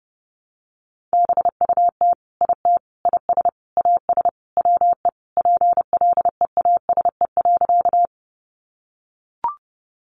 Morse code, heard as a keyed tone, about 30 wpm, 700 Hz. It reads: BUT IT IS AS WE PLEASE.
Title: Sentences from Top 1500 Words 30wpm